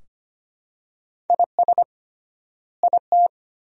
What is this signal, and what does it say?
Morse code, heard as a keyed tone, about 25 wpm, 700 Hz. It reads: IS IT
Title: 3x Word-Spacing - Sentences from Top 200 Words - 25wpm